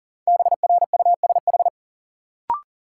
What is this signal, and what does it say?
Morse code, heard as a keyed tone, about 37 wpm, 700 Hz. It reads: BRUSH
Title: Sentences from Top 1100 Words - New Word Drill Alt - 40wpm